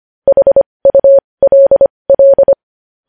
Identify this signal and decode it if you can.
Morse code, heard as a keyed tone, about 25 wpm, spaced as for 20 wpm, 550 Hz. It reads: HULL
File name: podcast / News headlines-1x-25x20